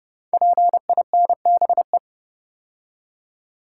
Morse code, heard as a keyed tone, about 30 wpm, 700 Hz. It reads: PINBE
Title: Code Groups - 5 Letter:Numbers - 30wpm